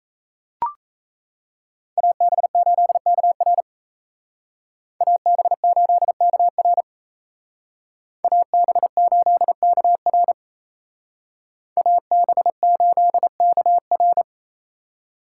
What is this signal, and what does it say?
Morse code, heard as a keyed tone, about 36 wpm, 700 Hz. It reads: AB8KR AB8KR AB8KR AB8KR